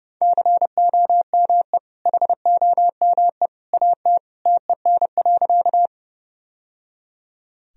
Morse code, heard as a keyed tone, about 30 wpm, 700 Hz. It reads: COME HOME AT TEN.
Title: Sentences from Top 1000 Words - Review Part 2 - 30wpm